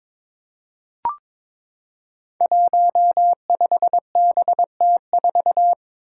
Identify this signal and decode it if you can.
Morse code, heard as a keyed tone, about 22 wpm, 700 Hz. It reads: 15BT4
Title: Five Challenging Letter-Number Combinations 22wpm